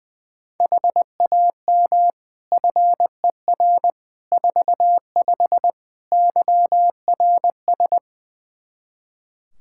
Morse code, heard as a keyed tone, about 20 wpm, 700 Hz. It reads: HAM FER 45 YRS
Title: K1USN SST QSO Practice 20wpm